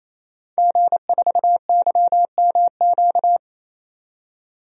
Morse code, heard as a keyed tone, about 28 wpm, 700 Hz. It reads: G4YMQ
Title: DXCC-100 Call Signs - 5 Characters - 28wpm